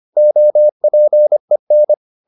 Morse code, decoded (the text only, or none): OPEN